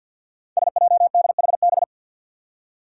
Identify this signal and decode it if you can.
Morse code, heard as a keyed tone, about 53 wpm, 700 Hz. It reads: SJDHB
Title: Five Challenging Letter-Number Combinations 50wpm